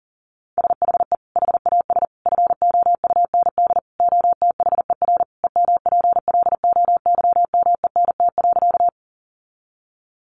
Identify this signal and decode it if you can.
Morse code, heard as a keyed tone, about 40 wpm, 700 Hz. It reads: SHE HAS FOUND OTHER EMPLOYMENT.
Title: Sentences from Top 1500 Words 40wpm